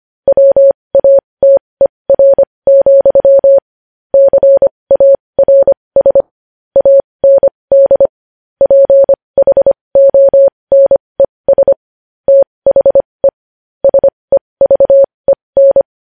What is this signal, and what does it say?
Morse code, heard as a keyed tone, about 25 wpm, spaced as for 20 wpm, 550 Hz. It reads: WATER, CARS AND PHONES THE SEVEN
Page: news-headlines-in-morse-code-at-20-wpm